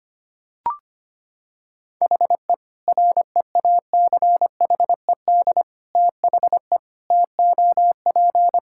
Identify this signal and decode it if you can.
Morse code, heard as a keyed tone, about 25 wpm, 700 Hz. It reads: HE REACHED THE TOP
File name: practice / 25wpm / Sentences with 3-Word Phrases - 25wpm